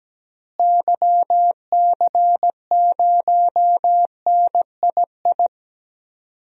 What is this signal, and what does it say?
Morse code, heard as a keyed tone, about 17 wpm, 700 Hz. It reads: YC0NII
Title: DXCC-100 Call Signs - 6 Characters - Repeated 2x - 17wpm